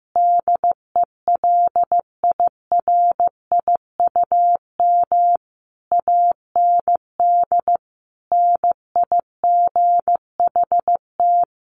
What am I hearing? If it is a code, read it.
Morse code, heard as a keyed tone, about 15 wpm, 700 Hz. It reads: DELIRIUM AND NIGHT
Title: A Sound Of Thunder 15wpm